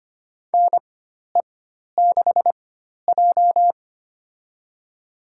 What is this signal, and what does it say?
Morse code, heard as a keyed tone, about 25 wpm, spaced as for 12 wpm, 700 Hz. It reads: NE6J
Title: USA Call Signs - 4 Characters - 4x Character Spacing - 25wpm